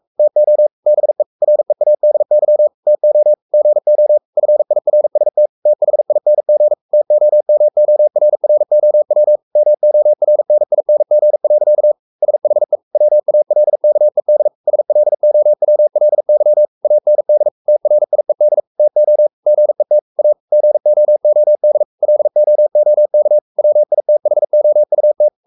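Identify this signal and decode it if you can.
Morse code, heard as a keyed tone, about 44 wpm, 600 Hz. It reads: TO BE READY TO GO FIRST THING TOMORROW MORNING. SHE WALKED SLOWLY AND TRIED TO GET A GOOD LOOK WITHOUT
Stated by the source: Morse code at 43 wpm sending Ken Follett's novel Never in English, from CW med Gnister.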